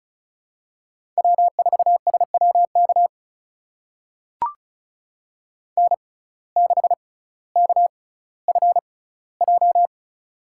Morse code, heard as a keyed tone, about 36 wpm, spaced as for 18 wpm, 700 Hz. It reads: W4SWK N6KFJ
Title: USA Call Signs - 5 Characters - 6x Character Spacing - 35wpm